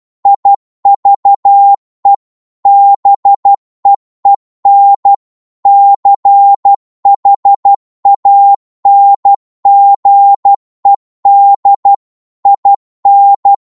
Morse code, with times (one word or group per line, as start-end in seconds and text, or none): 0.3-2.1 IVE
2.7-5.1 BEEN
5.7-11.9 CHANGED
12.5-13.5 IN